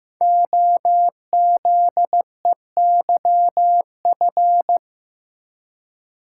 Morse code, heard as a keyed tone, about 15 wpm, 700 Hz. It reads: OZEYF
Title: Code Groups - 5 Letter:Numbers - 15wpm